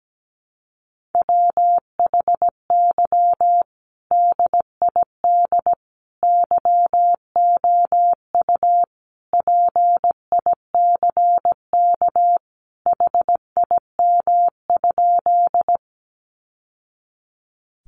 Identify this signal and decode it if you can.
Morse code, heard as a keyed tone, about 17 wpm, 700 Hz. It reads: WHY DID YOU PICK HIM?
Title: Sentences from Top 1000 Words 17wpm